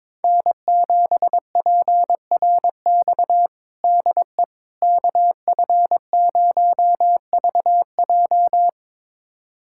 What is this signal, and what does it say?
Morse code, heard as a keyed tone, about 22 wpm, 700 Hz. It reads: N7PRX DE KF0VJ
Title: USA Callsign DE Callsign - 5 Characters - 22wpm